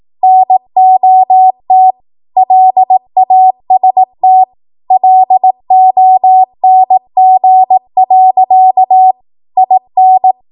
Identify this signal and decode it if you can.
Morse code, heard as a keyed tone, about 18 wpm, 750 Hz. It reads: NOT LAST LONG. IN
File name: Morse / 18WPM